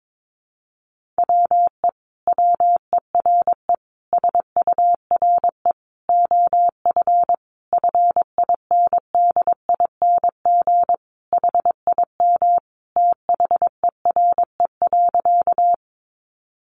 Morse code, heard as a keyed tone, about 22 wpm, 700 Hz. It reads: WE WERE SURE OF FINDING HIM THERE.
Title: Sentences from Top 1100 Words 22wpm